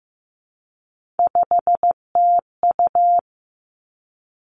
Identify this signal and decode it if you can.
Morse code, heard as a keyed tone, about 15 wpm, 700 Hz. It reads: 5TU
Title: Three Challenging Letter-Number Combinations 15wpm